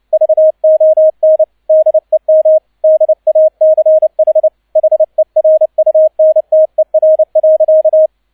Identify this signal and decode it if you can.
Morse code, heard as a keyed tone, about 29 wpm, 630 Hz, with radio noise. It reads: VON DEM DACH HERUNTER.
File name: 002 Zweiter Streich 30wpm